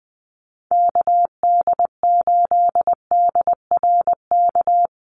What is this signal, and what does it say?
Morse code, heard as a keyed tone, about 20 wpm, 700 Hz. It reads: KD8DRK
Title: USA Call Signs 20wpm